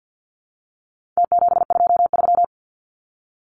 Morse code, heard as a keyed tone, about 50 wpm, 700 Hz. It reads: T723
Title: Numbers - Contest Style - 4-Digit 50wpm